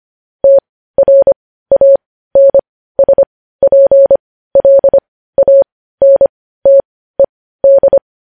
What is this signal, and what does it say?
Morse code, heard as a keyed tone, about 25 wpm, spaced as for 15 wpm, 550 Hz. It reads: TRANSPLANTED
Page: news-headlines-in-morse-code-at-15-wpm